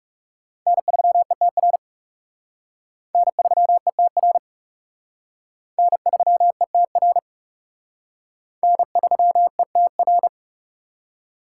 Morse code, heard as a keyed tone, about 36 wpm, 700 Hz. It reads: N3ETR N3ETR N3ETR N3ETR
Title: Speed Racing - USA 1x3 Call Signs 30wpm